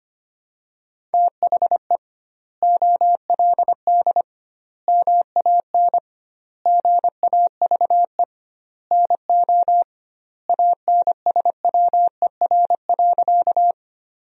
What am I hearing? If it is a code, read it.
Morse code, heard as a keyed tone, about 25 wpm, 700 Hz. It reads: THE OLD MAN GAVE NO ANSWER.
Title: 2x Word-Spacing - Sentences from Top 500 Words - 25wpm